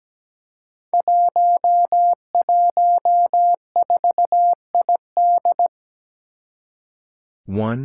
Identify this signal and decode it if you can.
Morse code, heard as a keyed tone, about 17 wpm, 700 Hz. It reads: 114ID
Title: Five Challenging Letter-Number Combinations 17wpm